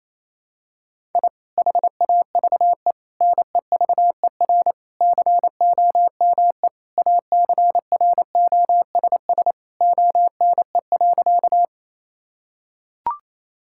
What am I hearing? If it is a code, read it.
Morse code, heard as a keyed tone, about 28 wpm, 700 Hz. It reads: I HAVE NEVER COME ACROSS ONE.
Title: Sentences from Top 1000 Words - Review Part 5 - 28wpm